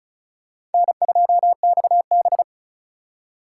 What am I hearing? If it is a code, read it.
Morse code, heard as a keyed tone, about 34 wpm, 700 Hz. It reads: N2XB
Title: USA Call Signs - 4 Characters - 2x Character Spacing - 35wpm